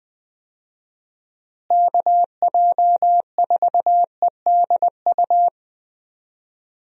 Morse code, heard as a keyed tone, about 21 wpm, 700 Hz. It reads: KJ4EDU
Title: USA Call Signs - 6 Characters - Repeated 3x - 20wpm